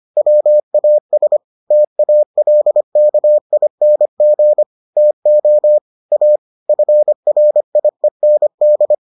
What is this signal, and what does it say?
Morse code, heard as a keyed tone, about 25 wpm, 600 Hz. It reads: WAS TALKING TO A FRIEND